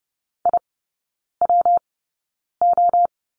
Morse code, heard as a keyed tone, about 30 wpm, spaced as for 9 wpm, 700 Hz. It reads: IWO
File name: Lesson 19e - Letter w - Three Characters 30wpm